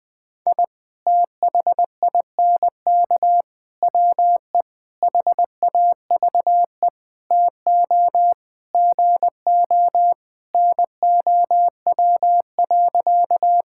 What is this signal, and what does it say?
Morse code, heard as a keyed tone, about 20 wpm, 700 Hz. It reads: I THINK WE HAVE TO GO NOW.
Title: Sentences with 5-Word Phrases - Spoken Beginning - 20wpm